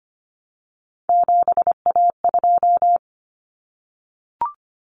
Morse code, heard as a keyed tone, about 25 wpm, 700 Hz. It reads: 7A2
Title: Common Cut Numbers - 3-Digit 25wpm